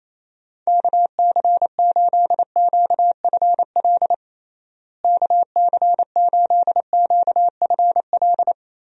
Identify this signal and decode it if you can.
Morse code, heard as a keyed tone, about 28 wpm, 700 Hz. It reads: KC8QFL KC8QFL
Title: USA Call Signs - 6 Characters - Repeated 3x - 28wpm